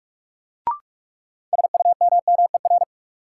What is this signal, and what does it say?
Morse code, heard as a keyed tone, about 46 wpm, 700 Hz. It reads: SUMMER
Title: Sentences from Top 1000 Words - Review Part 6 - New Word Drill - 45wpm